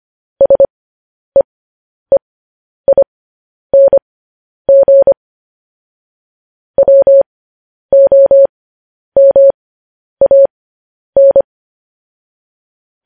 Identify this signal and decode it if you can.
Morse code, heard as a keyed tone, about 25 wpm, spaced as for 10 wpm, 550 Hz. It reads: SEEING WOMAN